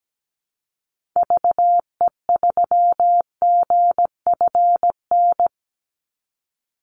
Morse code, heard as a keyed tone, about 17 wpm, 700 Hz. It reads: VE3GFN
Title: Call Signs - 17wpm